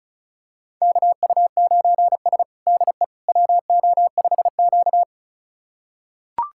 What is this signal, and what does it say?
Morse code, heard as a keyed tone, about 36 wpm, 700 Hz. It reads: KU9S DE WO5Q
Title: USA Callsign DE Callsign - 4 Characters - 35wpm